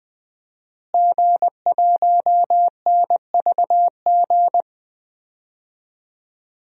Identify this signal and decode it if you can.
Morse code, heard as a keyed tone, about 20 wpm, 700 Hz. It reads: G1NVG
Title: DXCC-100 Call Signs - 5 Characters - Repeated 2x - 20wpm